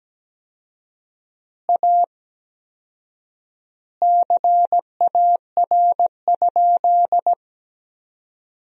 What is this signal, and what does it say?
Morse code, heard as a keyed tone, about 17 wpm, 700 Hz. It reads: A CAR?
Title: 4x Word-Spacing - Sentences from Top 300 Words - 17wpm